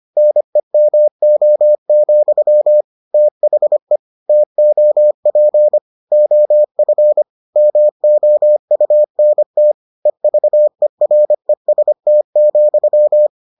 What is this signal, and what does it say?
Morse code, heard as a keyed tone, about 25 wpm, 600 Hz. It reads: NEMO, THE TOP OF MOUNT EVEREST,